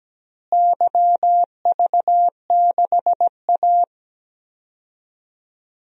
Morse code, heard as a keyed tone, about 17 wpm, 700 Hz. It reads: YV6A
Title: DXCC-100 Call Signs - 4 Characters - Repeated 2x - 17wpm